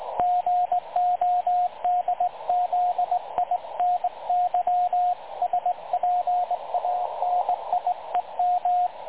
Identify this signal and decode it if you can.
Morse code, heard as a keyed tone, about 21 wpm, 720 Hz, with radio noise. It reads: GODZINY SPANIEM
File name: dlazaawansowanych / nagraniazpasma / websdr recording 3565.0kHz